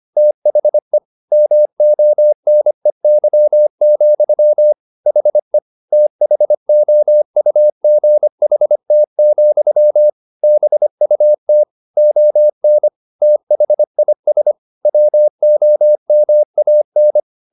Morse code, heard as a keyed tone, about 25 wpm, 600 Hz. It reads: THE MONEY, HE THOUGHT, BUT ON THIS WOMAN